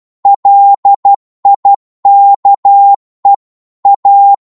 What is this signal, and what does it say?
Morse code, heard as a keyed tone, about 12 wpm, 800 Hz. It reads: LIKE A